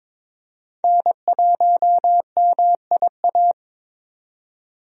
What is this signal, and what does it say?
Morse code, heard as a keyed tone, about 21 wpm, 700 Hz. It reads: N1MIA
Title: USA Call Signs - 5 Characters - 22wpm